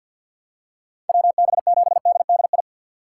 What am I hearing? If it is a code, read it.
Morse code, heard as a keyed tone, about 50 wpm, 700 Hz. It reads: WB7DDI
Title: USA Call Signs - 6 Characters - 4x Character Spacing - 50wpm